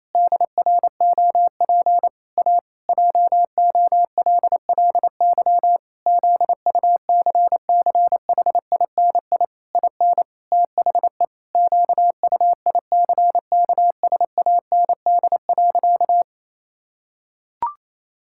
Morse code, heard as a keyed tone, about 27 wpm, 700 Hz. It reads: DROP A JOLLY ZUCCHINI IN THE QUICKSAND.
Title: Pangram Challenge v2 - 28wpm